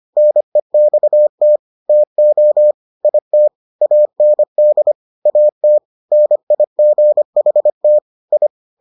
Morse code, heard as a keyed tone, about 25 wpm, 600 Hz. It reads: NEXT TO IT AND AT NIGHT I